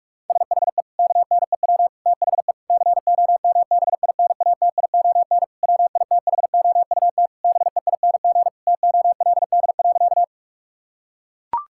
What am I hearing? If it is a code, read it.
Morse code, heard as a keyed tone, about 41 wpm, 700 Hz. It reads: SHE KNEW THE COMBINATION WITHOUT BEING TOLD.
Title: Sentences from Top 1000 Words - Review Part 6 - 45wpm